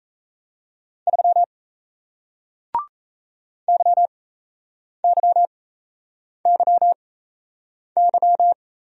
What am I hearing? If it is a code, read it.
Morse code, heard as a keyed tone, about 37 wpm, spaced as for 8 wpm, 700 Hz. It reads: 3 YYYY